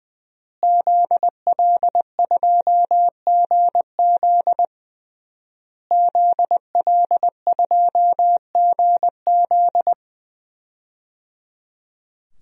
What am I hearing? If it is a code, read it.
Morse code, heard as a keyed tone, about 20 wpm, 700 Hz. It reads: ZL2GZ ZL2GZ
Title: DXCC-100 Call Signs - 5 Characters - Repeated 3x - 20wpm